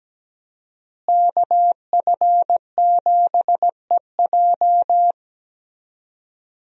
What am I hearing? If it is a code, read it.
Morse code, heard as a keyed tone, about 17 wpm, 700 Hz. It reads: KF7EJ
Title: USA Call Signs - 5 Characters - Repeated 3x - 17wpm